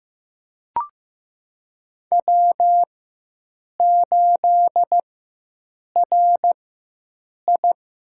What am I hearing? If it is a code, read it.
Morse code, heard as a keyed tone, about 15 wpm, spaced as for 7 wpm, 700 Hz. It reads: W8RI
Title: USA Call Signs - 4x Character Spacing - 15wpm